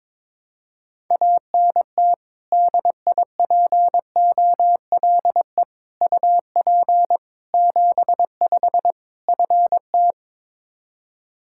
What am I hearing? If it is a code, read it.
Morse code, heard as a keyed tone, about 22 wpm, 700 Hz. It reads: ANT DIPOLE UP 75 FT
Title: K1USN SST QSO Practice 22wpm